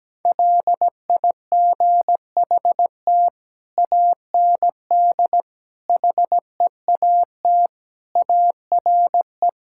Morse code, heard as a keyed tone, about 17 wpm, 700 Hz. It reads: LIGHT AND HEAT ARE
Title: Sentences from Top 1000 Words - Review Part 5 - 17wpm